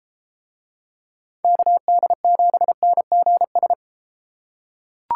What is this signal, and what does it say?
Morse code, heard as a keyed tone, about 32 wpm, 700 Hz. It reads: KD7NGS